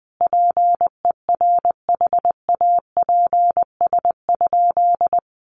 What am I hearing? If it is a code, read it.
Morse code, heard as a keyed tone, about 19 wpm, 700 Hz. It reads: PERHAPS?
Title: Sentences from Top 1200 Words 20wpm